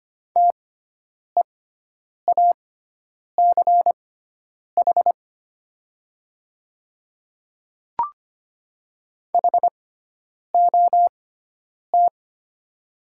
Morse code, heard as a keyed tone, about 25 wpm, spaced as for 9 wpm, 700 Hz. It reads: TEACH HOT
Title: 6x Character Spacing - Top 500 Words - 25wpm